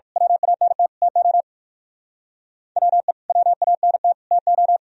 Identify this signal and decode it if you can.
Morse code, heard as a keyed tone, about 50 wpm, 700 Hz. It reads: WANT TO WE WANT TO
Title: Speed-Racing - Sets of 3 Words 45wpm